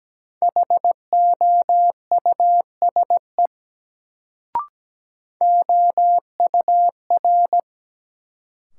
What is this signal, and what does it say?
Morse code, heard as a keyed tone, about 17 wpm, 700 Hz. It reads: HOUSE OUR
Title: Sentences from Top 200 Words - New Word Drill - 17wpm